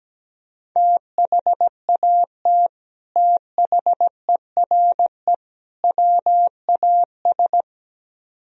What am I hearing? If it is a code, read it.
Morse code, heard as a keyed tone, about 17 wpm, 700 Hz. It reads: THAT THERE WAS
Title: Speed-Racing - Sets of 3 Words 17wpm